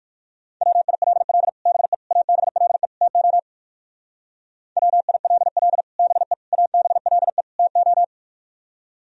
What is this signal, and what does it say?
Morse code, heard as a keyed tone, about 62 wpm, 700 Hz. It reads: WILL BE ABLE TO WILL BE ABLE TO